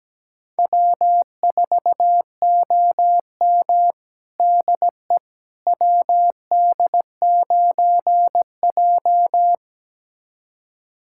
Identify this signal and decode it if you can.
Morse code, heard as a keyed tone, about 17 wpm, 700 Hz. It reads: W4OM DE WD9J
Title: USA Callsign DE Callsign - 4 Characters - 17wpm